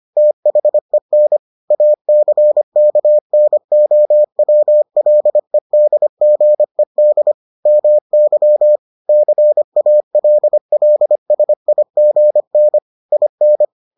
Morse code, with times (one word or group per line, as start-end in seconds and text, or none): 0.2-1.4 THEN
1.7-7.3 ACKNOWLEDGED
7.7-8.8 MY
9.1-12.8 CALLSIGN
13.1-13.6 IN